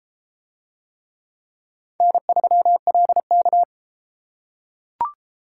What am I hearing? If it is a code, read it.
Morse code, heard as a keyed tone, about 31 wpm, 700 Hz. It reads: N3LK